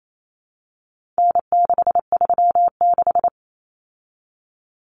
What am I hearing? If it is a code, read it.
Morse code, heard as a keyed tone, about 28 wpm, 700 Hz. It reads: N636
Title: Common Cut Numbers - 4-Digit 28wpm